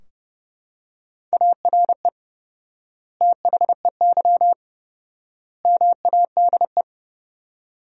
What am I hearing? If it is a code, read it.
Morse code, heard as a keyed tone, about 30 wpm, 700 Hz. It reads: ARE THEY MADE
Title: 4x Word-Spacing - Sentences from Top 400 Words - 30wpm